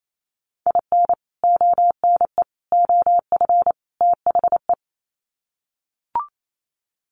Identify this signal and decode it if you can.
Morse code, heard as a keyed tone, about 28 wpm, 700 Hz. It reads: IN ONE OF THE
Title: Sets of 4 Words 28wpm